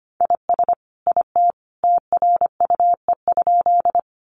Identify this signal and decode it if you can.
Morse code, heard as a keyed tone, about 25 wpm, 700 Hz. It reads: IS IT TRUE?
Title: Sentences from Top 400 Words 25wpm